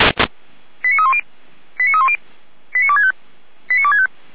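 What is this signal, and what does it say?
Five-tone selective call (ZVEI), heard as quick runs of five tones, tones of about 70 ms. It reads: 89310 89310 89265 89265
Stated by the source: ZVEI five-tone selective call 89160 received by SDR, Feuerwehr Orlamünde fire brigade.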